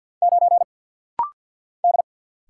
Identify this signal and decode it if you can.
Morse code, heard as a keyed tone, about 48 wpm, 690 Hz. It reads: 9 D
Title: Single Letter-Number - Rapid-Fire 50wpm